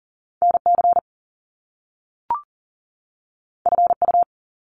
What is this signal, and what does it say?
Morse code, heard as a keyed tone, about 42 wpm, 700 Hz. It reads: NC FU
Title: Top Two Letter Combinations in Words 40wpm